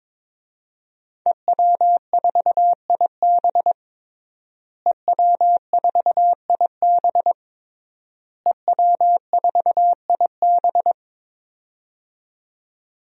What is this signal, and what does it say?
Morse code, heard as a keyed tone, about 22 wpm, 700 Hz. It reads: EW4IB EW4IB EW4IB
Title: DXCC-100 Call Signs - 5 Characters - Repeated 3x - 22wpm